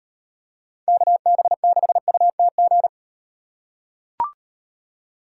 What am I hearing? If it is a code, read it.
Morse code, heard as a keyed tone, about 38 wpm, 700 Hz. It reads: KB6UTG